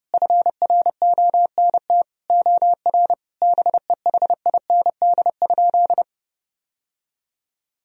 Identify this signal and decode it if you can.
Morse code, heard as a keyed tone, about 32 wpm, 700 Hz. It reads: FRONT OR BEHIND?
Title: Sentences from Top 1000 Words - Review Part 2 - 30wpm